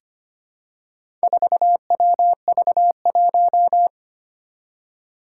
Morse code, heard as a keyed tone, about 25 wpm, 700 Hz. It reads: 4WV1
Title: Four Challenging Letter-Number Combinations 25wpm